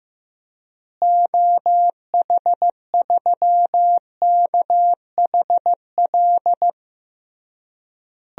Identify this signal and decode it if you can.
Morse code, heard as a keyed tone, about 15 wpm, 700 Hz. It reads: OH3KHL
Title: DXCC-100 Call Signs - 6 Characters - Repeated 3x - 15wpm